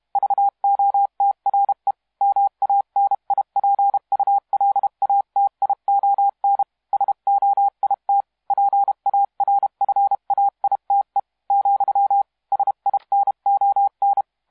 Morse code, heard as a keyed tone, about 31 wpm, 800 Hz. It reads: VOTRE MANIPULATION SOIT PARFAITE , SINON
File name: musiqueradio / cw hsc demo f6ddr